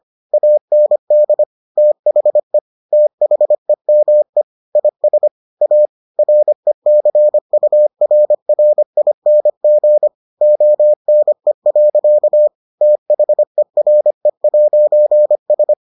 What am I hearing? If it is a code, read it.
Morse code, heard as a keyed tone, about 25 wpm, 600 Hz. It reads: AND THE THEME IS A RECURRING ONE. THERE'S